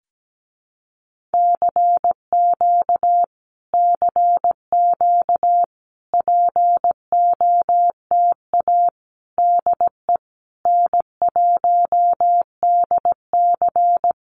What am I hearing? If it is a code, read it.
Morse code, heard as a keyed tone, about 17 wpm, 700 Hz. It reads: CQ CQ POTA DE N1DC
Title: Common QSO Phrases Encore 17wpm